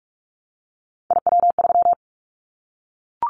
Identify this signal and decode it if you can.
Morse code, heard as a keyed tone, about 44 wpm, 700 Hz. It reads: IW3
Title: Callsign Prefixes for Top 50 DXCC - 45wpm